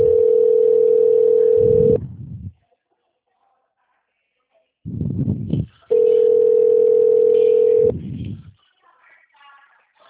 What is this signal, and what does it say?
Ringback tone (North America), 440 + 480 Hz.